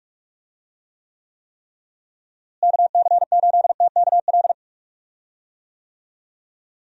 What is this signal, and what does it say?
Morse code, heard as a keyed tone, about 46 wpm, 700 Hz. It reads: KC8TKL
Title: USA Call Signs - 6 Characters - Repeated 3x - 45wpm